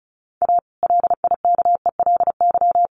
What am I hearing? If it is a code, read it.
Morse code, heard as a keyed tone, about 34 wpm, 700 Hz. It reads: A LIKELY
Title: Sentences from Top 600 Words 35wpm